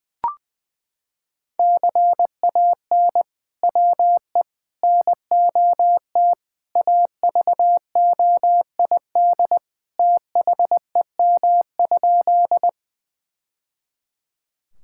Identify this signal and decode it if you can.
Morse code, heard as a keyed tone, about 20 wpm, 700 Hz. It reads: CAN WE NOT AVOID THEM?
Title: Sentences from Top 1000 Words - Review Part 6 - 20wpm